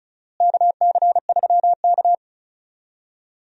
Morse code, heard as a keyed tone, about 36 wpm, 700 Hz. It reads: KC3K
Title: USA Call Signs - 4 Characters - Repeated 2x - 35wpm